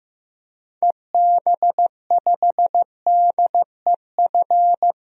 Morse code, heard as a keyed tone, about 15 wpm, 700 Hz. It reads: EB5DEF
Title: DXCC-100 Call Signs - 6 Characters - Repeated 3x - 15wpm